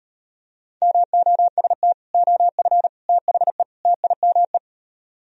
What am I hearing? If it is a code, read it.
Morse code, heard as a keyed tone, about 38 wpm, 700 Hz. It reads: MOST OF THE TIME